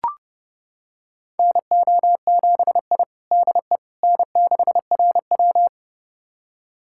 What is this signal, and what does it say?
Morse code, heard as a keyed tone, about 30 wpm, 700 Hz. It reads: NO7I DE N6RW
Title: USA Callsign DE Callsign - 4 Characters - 30wpm